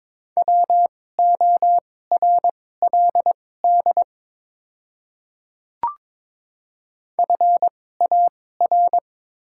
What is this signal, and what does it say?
Morse code, heard as a keyed tone, about 23 wpm, spaced as for 16 wpm, 700 Hz. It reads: WORLD FAR
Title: 2x Character Spacing - Top 500 Words - 22wpm